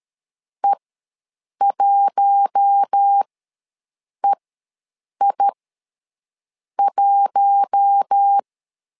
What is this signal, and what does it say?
Morse code, heard as a keyed tone, about 13 wpm, spaced as for 6 wpm, 800 Hz. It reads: E1EI1
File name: LEZIONI / 4 Q1 ws 12wpm 2129